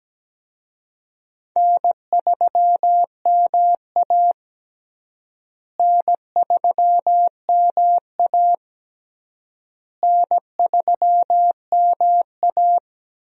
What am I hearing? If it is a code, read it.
Morse code, heard as a keyed tone, about 17 wpm, 700 Hz. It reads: N3MA N3MA N3MA
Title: USA Call Signs - 4 Characters - Repeated 3x - 17wpm